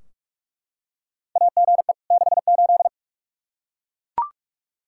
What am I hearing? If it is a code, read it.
Morse code, heard as a keyed tone, about 42 wpm, 700 Hz. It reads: AGE 68
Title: Practice QSO - Exchange 3 - More Info - 45wpm